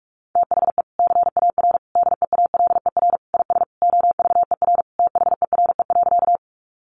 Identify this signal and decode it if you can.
Morse code, heard as a keyed tone, about 44 wpm, 700 Hz. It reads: THE CAR DEALER IS OVER THERE.
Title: Sentences from Top 1100 Words 45wpm